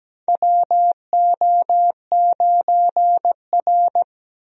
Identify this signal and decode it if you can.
Morse code, heard as a keyed tone, about 17 wpm, 700 Hz. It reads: WO9R
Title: USA Call Signs - 4 Characters - 17wpm